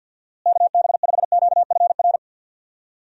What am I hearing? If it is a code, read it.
Morse code, heard as a keyed tone, about 52 wpm, 700 Hz. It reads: KB5QFR
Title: USA Call Signs - 6 Characters - Repeated 3x - 50wpm